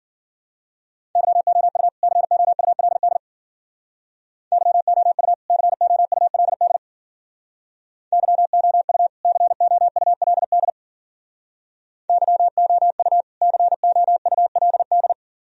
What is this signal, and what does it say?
Morse code, heard as a keyed tone, about 51 wpm, 700 Hz. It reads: YOU COULD YOU COULD YOU COULD YOU COULD